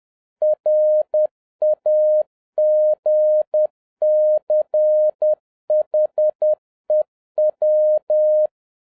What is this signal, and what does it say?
Morse code, heard as a keyed tone, about 10 wpm, 610 Hz. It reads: RAGCHEW
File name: 2011-07-mp3 / n21 GREETINGS FROM SINGAPORE 10wpm